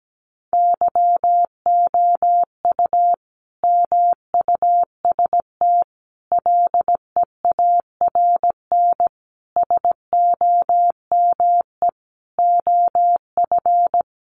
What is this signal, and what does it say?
Morse code, heard as a keyed tone, about 17 wpm, 700 Hz. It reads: YOU MUST LEARN SOME OF
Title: Sentences from Top 900 Words 17wpm